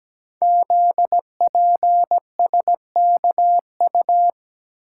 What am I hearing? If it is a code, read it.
Morse code, heard as a keyed tone, about 17 wpm, 700 Hz. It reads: ZPSKU
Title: Code Groups - 5 Letter:Numbers - 17wpm